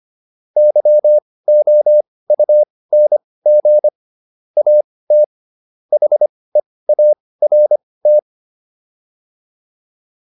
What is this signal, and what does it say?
Morse code, heard as a keyed tone, about 25 wpm, spaced as for 18 wpm, 600 Hz. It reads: YOUNG AT HEART